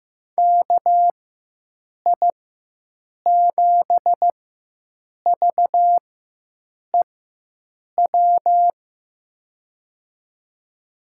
Morse code, heard as a keyed tone, about 15 wpm, spaced as for 7 wpm, 700 Hz. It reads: KI7VEW
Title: USA Call Signs - 6 Characters - 4x Character Spacing - 15wpm